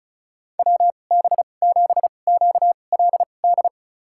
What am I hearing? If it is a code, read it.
Morse code, heard as a keyed tone, about 34 wpm, spaced as for 25 wpm, 700 Hz. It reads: WB7QLD
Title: USA Call Signs - 6 Characters - 2x Character Spacing - 35wpm